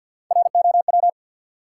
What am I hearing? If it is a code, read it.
Morse code, heard as a keyed tone, about 53 wpm, 700 Hz. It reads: ROW
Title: Three Letter Words - 50wpm